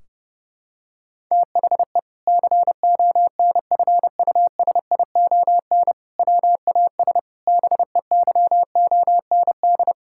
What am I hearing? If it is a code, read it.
Morse code, heard as a keyed tone, about 30 wpm, 700 Hz. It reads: THE CONFUSION WAS BEYOND
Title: Sentences from Top 1600 Words 30wpm